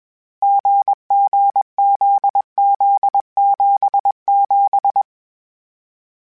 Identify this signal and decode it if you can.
Morse code, heard as a keyed tone, about 21 wpm, 800 Hz. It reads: GGZZ77